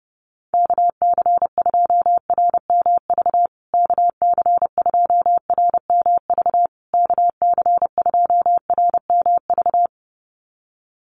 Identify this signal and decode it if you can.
Morse code, heard as a keyed tone, about 30 wpm, 700 Hz. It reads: KC2RMV KC2RMV KC2RMV
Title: USA Call Signs Repeated 3x 30wpm